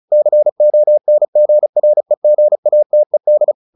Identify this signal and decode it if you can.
Morse code, heard as a keyed tone, about 34 wpm, 600 Hz. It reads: CONGREGATED
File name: the old man and the sea / the old man and the sea 1 35wpm